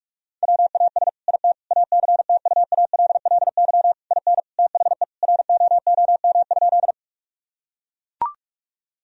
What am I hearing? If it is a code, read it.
Morse code, heard as a keyed tone, about 43 wpm, 700 Hz. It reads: WAS IT ACTUALLY IN THE ROOM?
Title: Sentences from Top 1000 Words - Review Part 4 - 45wpm